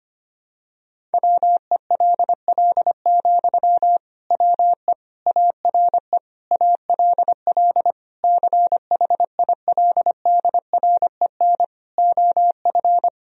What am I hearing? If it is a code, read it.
Morse code, heard as a keyed tone, about 25 wpm, 700 Hz. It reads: WELL, WE ARE ALL CHILDREN OF
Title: Sentences from Top 1000 Words - Review Part 4 - 25wpm